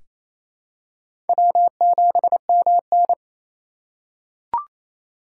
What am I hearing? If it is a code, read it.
Morse code, heard as a keyed tone, about 28 wpm, 700 Hz. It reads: W7MN
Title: USA Call Signs - 4 Characters - 3x Character Spacing - 28wpm